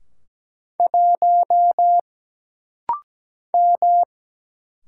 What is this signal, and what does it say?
Morse code, heard as a keyed tone, about 17 wpm, 700 Hz. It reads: 1 M
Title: Single Letter-Number Alt - Rapid-Fire 17wpm